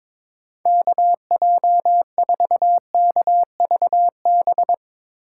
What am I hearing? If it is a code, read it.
Morse code, heard as a keyed tone, about 22 wpm, 700 Hz. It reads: KJ4KVB
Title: USA Call Signs - 6 Characters - 4x Character Spacing - 22wpm